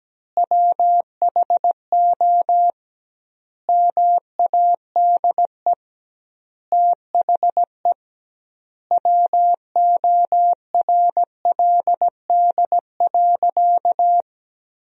Morse code, heard as a keyed tone, about 17 wpm, 700 Hz. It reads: WHO MADE THE WORLD.
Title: 2x Word-Spacing - Sentences from Top 200 Words - 17wpm